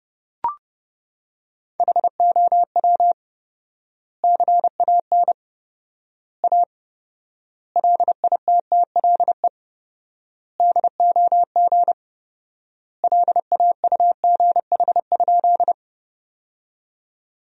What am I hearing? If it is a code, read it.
Morse code, heard as a keyed tone, about 32 wpm, 700 Hz. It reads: HOW CAN A LITTLE DOG LAUGH?
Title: 4x Word-Spacing - Sentences from Top 500 Words - 30wpm